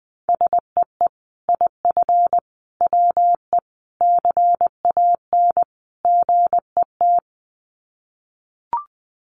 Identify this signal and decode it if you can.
Morse code, heard as a keyed tone, about 20 wpm, 700 Hz. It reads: SEE IF WE CAN GET
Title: Sets of 5 Words Encore2 20wpm